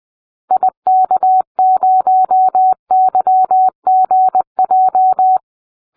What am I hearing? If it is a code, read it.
Morse code, heard as a keyed tone, about 20 wpm, 750 Hz. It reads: IK0YGJ